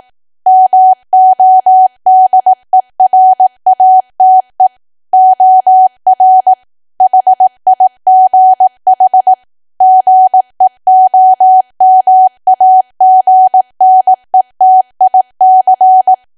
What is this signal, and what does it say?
Morse code, heard as a keyed tone, about 18 wpm, 750 Hz. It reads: MODERATE OR HIGH GEOMAGNETIC
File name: cwa / W1AW-Level2 / 18WPM